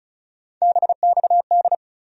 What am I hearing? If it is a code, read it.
Morse code, heard as a keyed tone, about 36 wpm, 700 Hz. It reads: BXD
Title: USA Call Sign Postfixes - 3 Letters 35wpm